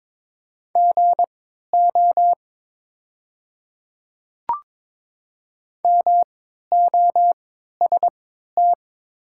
Morse code, heard as a keyed tone, about 22 wpm, spaced as for 12 wpm, 700 Hz. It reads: GO MOST
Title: 3x Character Spacing - Top 100 Words - 22wpm